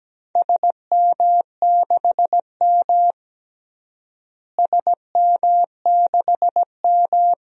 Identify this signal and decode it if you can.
Morse code, heard as a keyed tone, about 17 wpm, 700 Hz. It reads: SM6M SM6M
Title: DXCC-100 Call Signs - 4 Characters - Repeated 2x - 17wpm